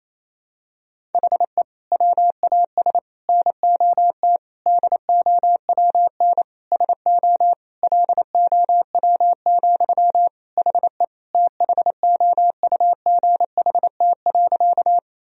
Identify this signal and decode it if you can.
Morse code, heard as a keyed tone, about 28 wpm, 700 Hz. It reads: HE WAS NOT DOWN SO LOW, HE THOUGHT.
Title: Sentences from Top 1000 Words - Review Part 1 - 28wpm